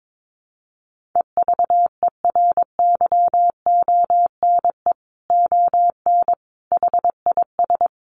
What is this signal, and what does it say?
Morse code, heard as a keyed tone, about 22 wpm, 700 Hz. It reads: EVERYONE ON HIS
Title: A Sound Of Thunder 22wpm